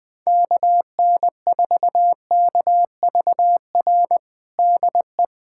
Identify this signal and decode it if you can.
Morse code, heard as a keyed tone, about 21 wpm, 700 Hz. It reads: KN4KVR DE
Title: USA Callsign DE Callsign 20wpm